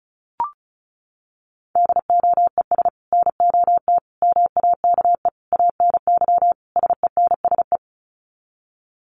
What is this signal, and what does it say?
Morse code, heard as a keyed tone, about 34 wpm, 700 Hz. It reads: DOES NOT MAKE ANY SENSE
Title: Sets of 5 Words Encore2 35wpm